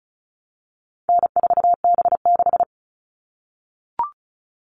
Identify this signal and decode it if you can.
Morse code, heard as a keyed tone, about 34 wpm, 700 Hz. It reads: N4B6
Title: All Cut Numbers - 4-Digit 35wpm